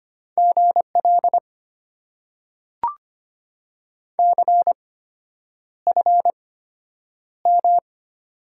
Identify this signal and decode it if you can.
Morse code, heard as a keyed tone, about 25 wpm, spaced as for 9 wpm, 700 Hz. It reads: GL CFM
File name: Top 90 QSO Elements - 8x Character Spacing - 25wpm